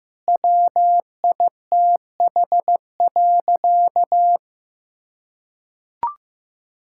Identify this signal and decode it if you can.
Morse code, heard as a keyed tone, about 15 wpm, 700 Hz. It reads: WITH.
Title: Sentences from Top 1000 Words - Review Part 5 - 15wpm